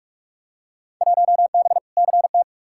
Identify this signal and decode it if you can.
Morse code, heard as a keyed tone, about 46 wpm, 700 Hz. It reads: 1B CT
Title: Field Day Exchanges Encore - 45wpm